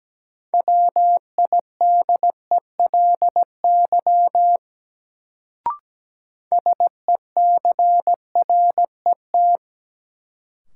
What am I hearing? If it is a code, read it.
Morse code, heard as a keyed tone, about 17 wpm, 700 Hz. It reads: WIDELY SECRET
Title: Sentences from Top 1200 Words - New Word Drill - 17wpm